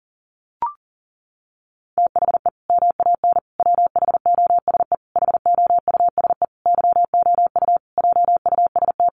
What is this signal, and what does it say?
Morse code, heard as a keyed tone, about 40 wpm, 700 Hz. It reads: THE MAN WHOSE HOUSE YOU JUST
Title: Sentences from Top 1200 Words 40wpm